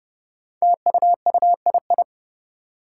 Morse code, heard as a keyed tone, about 31 wpm, 700 Hz. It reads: TUUII
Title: Five Challenging Letter-Number Combinations 30wpm